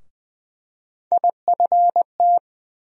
Morse code, heard as a keyed tone, about 20 wpm, 700 Hz. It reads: IFT
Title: USA Call Sign Postfixes - 3 Letters 20wpm